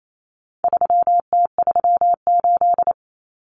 Morse code, heard as a keyed tone, about 28 wpm, 700 Hz. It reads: 3T38
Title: Common Cut Numbers - 4-Digit 28wpm